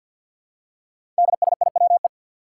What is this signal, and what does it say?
Morse code, heard as a keyed tone, about 50 wpm, 700 Hz. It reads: DSIWE
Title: Five Challenging Letter-Number Combinations 50wpm